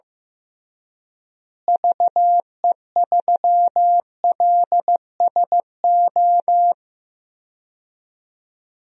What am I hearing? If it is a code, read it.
Morse code, heard as a keyed tone, about 15 wpm, 700 Hz. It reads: VE3LSO
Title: DXCC-100 Call Signs - 6 Characters - Repeated 2x - 15wpm